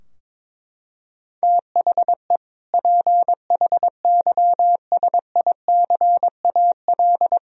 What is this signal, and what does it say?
Morse code, heard as a keyed tone, about 22 wpm, 700 Hz. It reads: THE PHYSICAL
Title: Sentences from Top 1000 Words - Review Part 4 - 22wpm